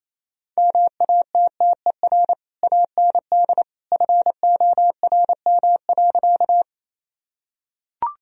Morse code, heard as a keyed tone, about 27 wpm, 700 Hz. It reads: MATTER AND FORM.
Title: Sentences from Top 1000 Words - Review Part 1 - 28wpm